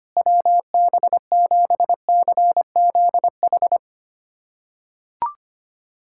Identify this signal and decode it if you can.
Morse code, heard as a keyed tone, about 25 wpm, 700 Hz. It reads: WB7CZH
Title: USA Callsign DE Callsign 25wpm